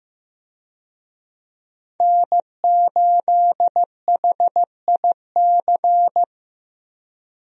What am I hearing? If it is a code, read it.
Morse code, heard as a keyed tone, about 15 wpm, 700 Hz. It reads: N8HIC
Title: USA Call Signs - 5 Characters - 15wpm